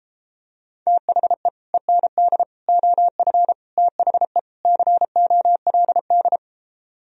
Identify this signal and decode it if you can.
Morse code, heard as a keyed tone, about 33 wpm, 700 Hz. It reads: THE END OF THE COLD